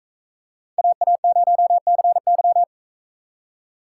Morse code, heard as a keyed tone, about 45 wpm, 700 Hz. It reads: AA0CY